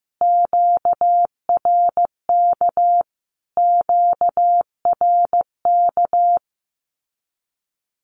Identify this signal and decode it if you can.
Morse code, heard as a keyed tone, about 15 wpm, 700 Hz. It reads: QRK QRK
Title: Q-Codes Repeated 3x 15wpm